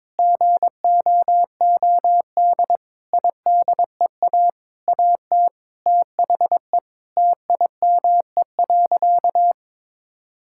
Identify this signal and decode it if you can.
Morse code, heard as a keyed tone, about 22 wpm, 700 Hz. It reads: GOOD IDEA AT THE TIME.
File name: practice / 22wpm / Sentences with 8-Word Phrases - 22wpm